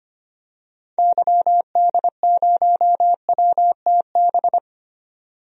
Morse code, heard as a keyed tone, about 25 wpm, 700 Hz. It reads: YD0WTB
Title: DXCC-100 Call Signs - 6 Characters - 25wpm